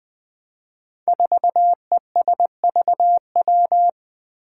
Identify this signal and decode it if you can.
Morse code, heard as a keyed tone, about 20 wpm, 700 Hz. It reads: 4ESVW
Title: Five Challenging Letter-Number Combinations 20wpm